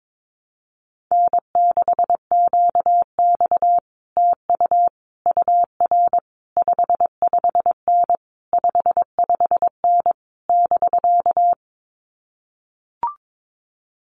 Morse code, heard as a keyed tone, about 22 wpm, 700 Hz. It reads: N6QX TU UR 55N 55N <BK>
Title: Common QSO Phrases Encore 22wpm